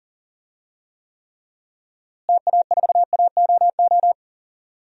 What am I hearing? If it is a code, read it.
Morse code, heard as a keyed tone, about 37 wpm, 700 Hz. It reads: TA4AOO
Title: DXCC-100 Call Signs - 6 Characters - Repeated 3x - 40wpm